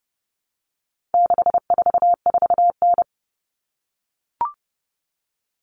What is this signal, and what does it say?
Morse code, heard as a keyed tone, about 30 wpm, 700 Hz. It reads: 644N
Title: All Cut Numbers - 4-Digit 30wpm